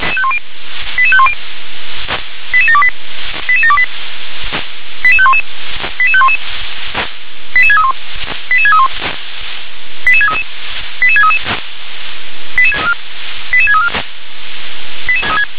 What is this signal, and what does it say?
Five-tone selective call (ZVEI), heard as quick runs of five tones, tones of about 70 ms. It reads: 88510 88728 88728 88511 88511 88631 88631 88633 88633 88634